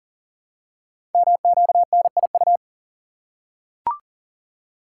Morse code, heard as a keyed tone, about 39 wpm, 700 Hz. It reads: MQNIU
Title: Code Groups - 5 Letters - 40wpm